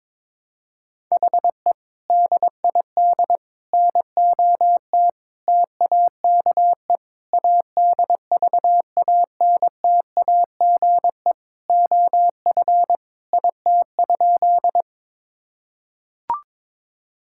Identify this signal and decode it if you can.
Morse code, heard as a keyed tone, about 21 wpm, 700 Hz. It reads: HE DID NOT TAKE ADVANTAGE OF IT?
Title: Sentences from Top 1000 Words - Review Part 6 - 22wpm